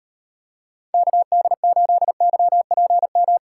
Morse code, heard as a keyed tone, about 38 wpm, 700 Hz. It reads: KD8YPM